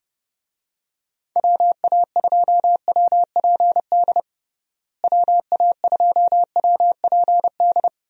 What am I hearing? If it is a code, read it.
Morse code, heard as a keyed tone, about 32 wpm, 700 Hz. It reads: WA2WPD WA2WPD
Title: USA Call Signs - 6 Characters - Repeated 2x - 30wpm